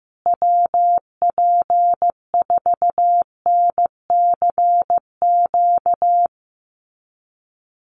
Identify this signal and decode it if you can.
Morse code, heard as a keyed tone, about 15 wpm, 700 Hz. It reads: WP4NCQ
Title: USA Call Signs Repeated 2x 15wpm